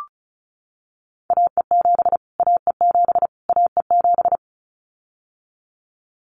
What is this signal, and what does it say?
Morse code, heard as a keyed tone, about 34 wpm, 700 Hz. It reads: AE7 AE7 AE7
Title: Callsign Prefixes for Top 25 DXCC - Repeated 3x - 35wpm